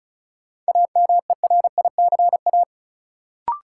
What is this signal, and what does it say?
Morse code, heard as a keyed tone, about 34 wpm, 700 Hz. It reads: AMERICA
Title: Sets of 5 Words - New Word Drill - 35wpm